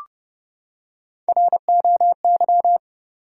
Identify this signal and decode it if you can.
Morse code, heard as a keyed tone, about 30 wpm, 700 Hz. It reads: ROY
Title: USA Common Names - Max 3 Letters - 30wpm